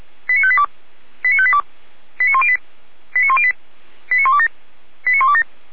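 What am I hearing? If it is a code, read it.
Five-tone selective call (ZVEI), heard as quick runs of five tones, tones of about 70 ms. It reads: 89582 89582 89108 89108 89127 89127
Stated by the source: Received on an SDR.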